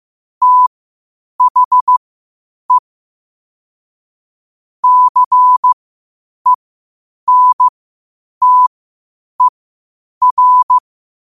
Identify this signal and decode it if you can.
Morse code, heard as a keyed tone, about 15 wpm, spaced as for 8 wpm, 1000 Hz. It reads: THE CENTER
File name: qotdfiles / 8wpm